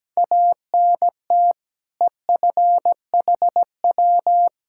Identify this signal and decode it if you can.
Morse code, heard as a keyed tone, about 17 wpm, 700 Hz. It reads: ANT EFHW
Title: Practice QSO - Exchange 2 - Info - 17wpm